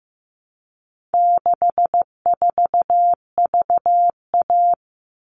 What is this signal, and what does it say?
Morse code, heard as a keyed tone, about 15 wpm, 700 Hz. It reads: 64VA
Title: All Cut Numbers - 4-Digit 15wpm